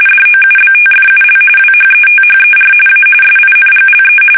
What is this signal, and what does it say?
RTTY, about 45.45 baud. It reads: DOUBTFUL WHETHER A SINGLE
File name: fsk850 / battleships